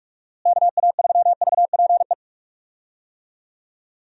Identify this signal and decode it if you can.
Morse code, heard as a keyed tone, about 44 wpm, 700 Hz. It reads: KA3VPE
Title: USA Call Signs - 6 Characters - Repeated 3x - 45wpm